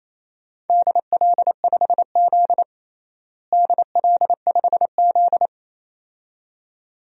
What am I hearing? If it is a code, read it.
Morse code, heard as a keyed tone, about 28 wpm, 700 Hz. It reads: DL5Z DL5Z
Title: DXCC-100 Call Signs - 4 Characters - Repeated 2x - 28wpm